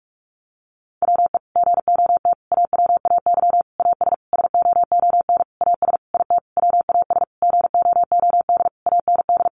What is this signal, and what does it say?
Morse code, heard as a keyed tone, about 44 wpm, 700 Hz. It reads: WE GOT AWAY AS SOON AS IT WAS GOOD AND
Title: Sentences from Top 600 Words 45wpm